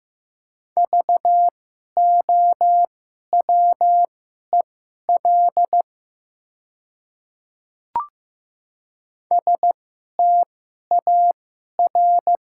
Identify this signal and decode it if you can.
Morse code, heard as a keyed tone, about 15 wpm, spaced as for 11 wpm, 700 Hz. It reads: VOWEL STAR
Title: 2x Character Spacing - Top 500 Words - 15wpm